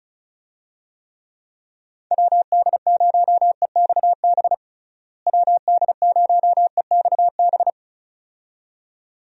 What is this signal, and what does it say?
Morse code, heard as a keyed tone, about 34 wpm, 700 Hz. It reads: WD0EXB WD0EXB
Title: USA Call Signs - 6 Characters - Repeated 2x - 35wpm